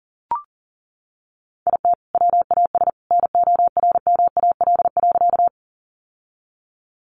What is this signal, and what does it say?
Morse code, heard as a keyed tone, about 40 wpm, 700 Hz. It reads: IT WAS NORMAL.
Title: Sentences from Top 1200 Words 40wpm